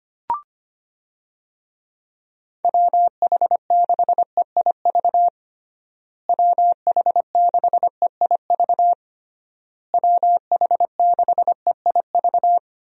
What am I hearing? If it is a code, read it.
Morse code, heard as a keyed tone, about 25 wpm, 700 Hz. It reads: WH6EIV WH6EIV WH6EIV
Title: USA Call Signs - 6 Characters - Repeated 3x - 25wpm